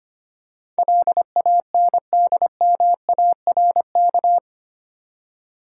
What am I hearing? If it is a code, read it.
Morse code, heard as a keyed tone, about 25 wpm, 700 Hz. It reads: LANDMARK
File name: 8 Letter Words - First 3 Letters Given - 25wpm